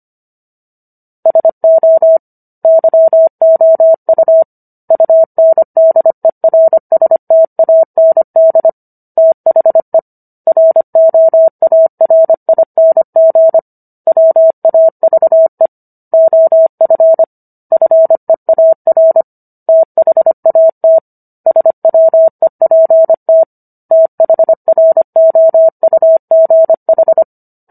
Morse code, heard as a keyed tone, about 25 wpm, 650 Hz. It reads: SO YOU UNDERSTAND THE ROARING WAVE OF FEAR THAT SWEPT THROUGH